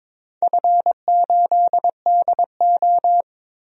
Morse code, heard as a keyed tone, about 22 wpm, 700 Hz. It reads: F8DO
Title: DXCC-100 Call Signs - 4 Characters - Repeated 3x - 22wpm